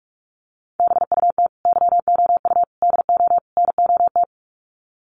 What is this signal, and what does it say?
Morse code, heard as a keyed tone, about 44 wpm, 700 Hz. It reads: BUT YOU DO NOT
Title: Sets of 4 Words 45wpm